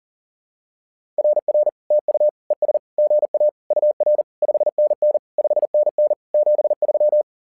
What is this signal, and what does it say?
Morse code, heard as a keyed tone, about 40 wpm, 600 Hz. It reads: RR TU ES GA UR 5NN 5NN 73
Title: SOTA QSO Practice 40wpm